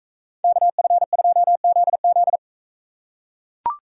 Morse code, heard as a keyed tone, about 40 wpm, 700 Hz. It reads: KF2ZZ